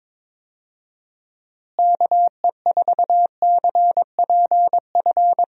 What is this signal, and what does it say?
Morse code, heard as a keyed tone, about 22 wpm, 700 Hz. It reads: KE4CPF
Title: USA Call Signs - 6 Characters - Repeated 2x - 22wpm